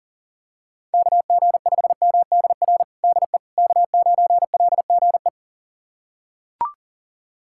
Morse code, heard as a keyed tone, about 37 wpm, 700 Hz. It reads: KG5MDR DE K9LGE
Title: USA Callsign DE Callsign 40wpm